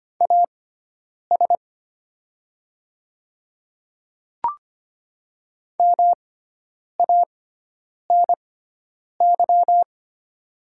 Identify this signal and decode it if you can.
Morse code, heard as a keyed tone, about 25 wpm, spaced as for 9 wpm, 700 Hz. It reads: AS MANY
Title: 6x Character Spacing - Top 100 Words - 25wpm